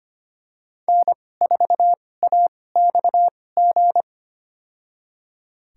Morse code, heard as a keyed tone, about 25 wpm, spaced as for 18 wpm, 700 Hz. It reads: N4AXG
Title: USA Call Signs - 5 Characters - 2x Character Spacing - 25wpm